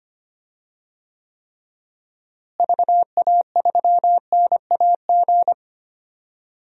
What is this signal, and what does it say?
Morse code, heard as a keyed tone, about 25 wpm, 700 Hz. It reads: VA3NAG
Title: DXCC-100 Call Signs - 6 Characters - Repeated 3x - 25wpm